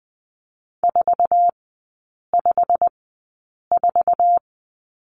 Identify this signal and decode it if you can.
Morse code, heard as a keyed tone, about 20 wpm, spaced as for 8 wpm, 690 Hz. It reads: 454
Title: 087 Lesson 15e - Number 5 - Three Characters 20wpm